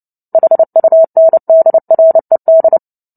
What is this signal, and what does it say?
Morse code, heard as a keyed tone, about 30 wpm, 650 Hz. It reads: HUNDRED
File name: Book Two / 03-Chapter 3 - 29 WPM